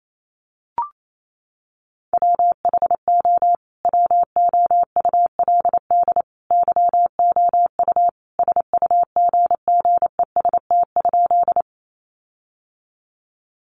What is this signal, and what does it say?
Morse code, heard as a keyed tone, about 28 wpm, 700 Hz. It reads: WHO WOULD YOU SUGGEST?
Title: Sentences from Top 1300 Words 28wpm